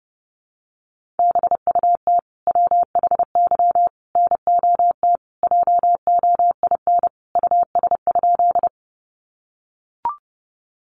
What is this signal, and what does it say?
Morse code, heard as a keyed tone, about 30 wpm, 700 Hz. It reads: BUT WHY NOT JOIN US?
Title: Sentences from Top 1400 Words 30wpm